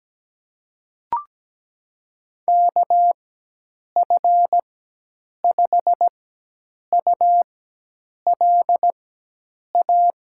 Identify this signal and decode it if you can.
Morse code, heard as a keyed tone, about 17 wpm, spaced as for 8 wpm, 700 Hz. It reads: KF5ULA
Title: USA Call Signs - 6 Characters - 4x Character Spacing - 17wpm